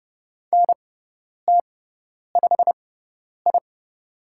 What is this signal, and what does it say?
Morse code, heard as a keyed tone, about 30 wpm, spaced as for 10 wpm, 700 Hz. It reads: NT5I
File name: Lesson 16g - Letter c - Call Signs 30wpm